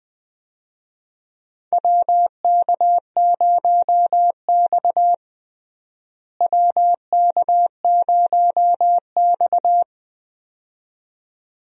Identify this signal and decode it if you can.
Morse code, heard as a keyed tone, about 20 wpm, 700 Hz. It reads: WK0X WK0X
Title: USA Call Signs - 4 Characters - Repeated 2x - 20wpm